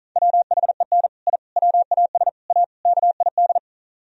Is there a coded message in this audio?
Yes: Morse code from 0.2 to 3.6 s, heard as a keyed tone, about 46 wpm, 700 Hz. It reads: WHEN I WAS A KID